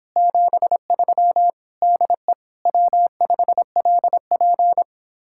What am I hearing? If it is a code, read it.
Morse code, heard as a keyed tone, about 29 wpm, 700 Hz. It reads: 73 DE W5LP